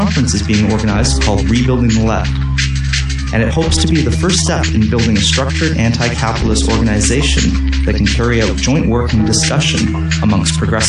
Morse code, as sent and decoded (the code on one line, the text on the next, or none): none